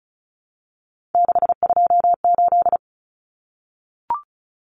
Morse code, heard as a keyed tone, about 34 wpm, 700 Hz. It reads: 628
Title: Common Cut Numbers - 3-Digit 35wpm